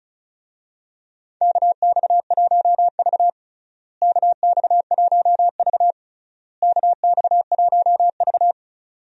Morse code, heard as a keyed tone, about 36 wpm, 700 Hz. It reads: KX1V KX1V KX1V
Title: USA Call Signs - 4 Characters - Repeated 3x - 35wpm